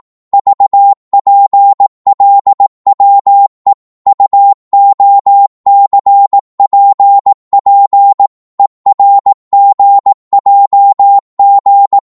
Morse code, heard as a keyed tone, about 18 wpm, 800 Hz. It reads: VPLWE UOCPP ERGJG
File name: Morse / ltr-18wpm